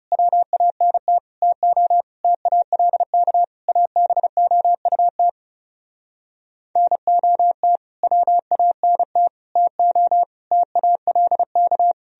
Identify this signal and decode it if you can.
Morse code, heard as a keyed tone, about 40 wpm, 700 Hz. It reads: WANT TO TALK ABOUT NOT WANT TO TALK